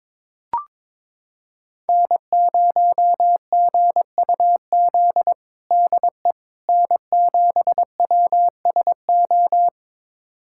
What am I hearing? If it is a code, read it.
Morse code, heard as a keyed tone, about 22 wpm, 700 Hz. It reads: N0GUZ DE N7WSO
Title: USA Callsign DE Callsign - 5 Characters - 22wpm